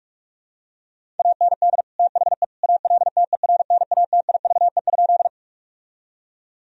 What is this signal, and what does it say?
Morse code, heard as a keyed tone, about 43 wpm, 700 Hz. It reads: AND THE ALTERNATIVE?
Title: Sentences from Top 1000 Words - Review Part 4 - 45wpm